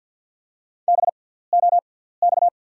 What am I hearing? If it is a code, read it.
Morse code, heard as a keyed tone, about 48 wpm, spaced as for 17 wpm, 690 Hz. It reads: BOX
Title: 6x Character Spacing - Top 500 Words - 50wpm